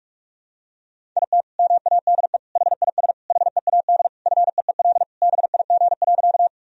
Morse code, heard as a keyed tone, about 41 wpm, 700 Hz. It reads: IT MADE HIS HEAD FEEL BIG.
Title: Sentences from Top 1000 Words - Review Part 1 - 45wpm